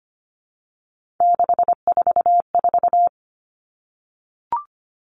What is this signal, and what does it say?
Morse code, heard as a keyed tone, about 25 wpm, 700 Hz. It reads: 644
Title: Numbers - 3-Digit 25wpm